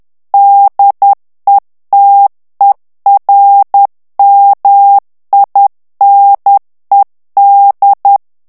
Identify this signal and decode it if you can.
Morse code, heard as a keyed tone, about 10 wpm, 800 Hz. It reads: DETERMINED
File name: ANGLAIS texte clair 3 vitesse 10 mots minute